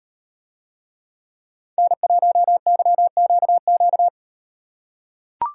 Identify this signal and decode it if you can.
Morse code, heard as a keyed tone, about 37 wpm, 700 Hz. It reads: N1YQQ